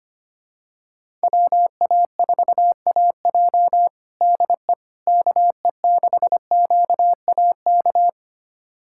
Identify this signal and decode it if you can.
Morse code, heard as a keyed tone, about 25 wpm, 700 Hz. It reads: WA4AJ DE KE6QAK
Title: USA Callsign DE Callsign 25wpm